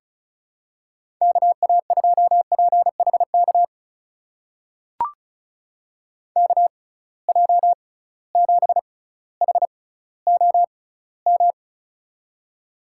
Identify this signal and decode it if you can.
Morse code, heard as a keyed tone, about 34 wpm, spaced as for 18 wpm, 700 Hz. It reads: KA2PHK KJ7HOM
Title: USA Call Signs - 6 Characters - 6x Character Spacing - 35wpm